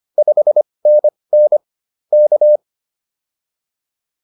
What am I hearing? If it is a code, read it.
Morse code, heard as a keyed tone, about 25 wpm, spaced as for 20 wpm, 600 Hz. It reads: 5NN K